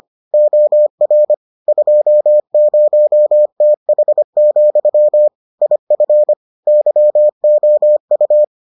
Morse code, heard as a keyed tone, about 25 wpm, 600 Hz. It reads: OR 20TH, IF YOU